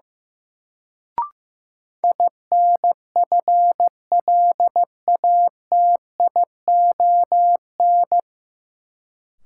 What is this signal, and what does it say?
Morse code, heard as a keyed tone, about 15 wpm, 700 Hz. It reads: INFLATION
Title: Sentences from Top 1100 Words - New Word Drill Alt - 15wpm